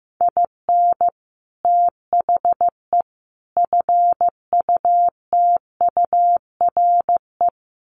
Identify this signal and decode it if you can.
Morse code, heard as a keyed tone, about 15 wpm, 700 Hz. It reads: IN THE FUTURE
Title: Sets of 3 Words 15wpm